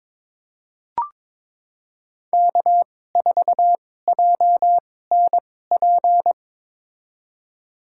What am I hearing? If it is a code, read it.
Morse code, heard as a keyed tone, about 22 wpm, spaced as for 16 wpm, 700 Hz. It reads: K4JNP
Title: USA Call Signs - 2x Character Spacing - 22wpm